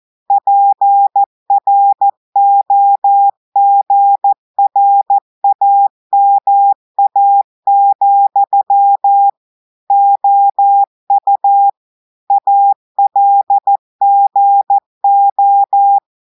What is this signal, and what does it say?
Morse code, heard as a keyed tone, about 14 wpm, 800 Hz. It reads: PROGRAMA, OU ALGO
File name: audios / portugais / portugais texte en clair 2 vitesse 14 mots minutes